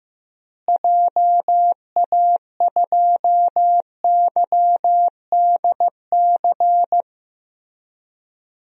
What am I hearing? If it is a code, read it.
Morse code, heard as a keyed tone, about 15 wpm, 700 Hz. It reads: JA2YDC
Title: DXCC-100 Call Signs - 6 Characters - Repeated 3x - 15wpm